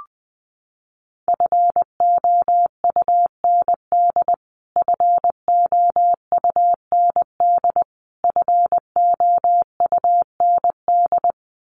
Morse code, heard as a keyed tone, about 20 wpm, 700 Hz. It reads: FOUND FOUND FOUND
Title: Top 500 Words Repeated 20wpm